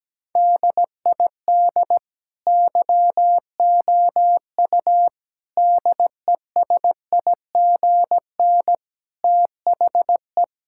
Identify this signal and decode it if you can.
Morse code, heard as a keyed tone, about 17 wpm, 700 Hz. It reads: DID YOU DESIGN THE
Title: Sentences from Top 1000 Words - Review Part 5 - 17wpm